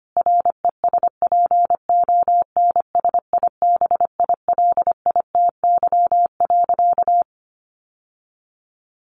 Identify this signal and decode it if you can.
Morse code, heard as a keyed tone, about 25 wpm, 700 Hz. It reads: RESPONSIBILITY.
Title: Sentences from Top 1500 Words 25wpm